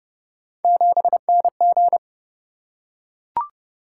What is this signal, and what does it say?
Morse code, heard as a keyed tone, about 29 wpm, 700 Hz. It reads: 7NG